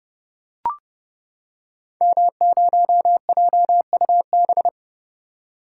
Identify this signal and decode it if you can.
Morse code, heard as a keyed tone, about 32 wpm, 700 Hz. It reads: M0JUB
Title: Code Groups - 5 Letter:Numbers - 30wpm